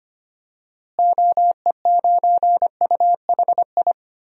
Morse code, heard as a keyed tone, about 25 wpm, 700 Hz. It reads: OE9UHI
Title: DXCC-100 Call Signs - 6 Characters - Repeated 2x - 25wpm